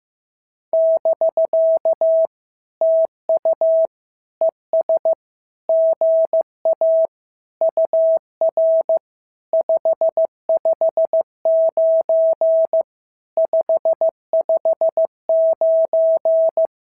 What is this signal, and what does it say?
Morse code, heard as a keyed tone, about 15 wpm, 650 Hz. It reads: <BK> TU ES GA UR 559 559
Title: SOTA QSO Practice 15wpm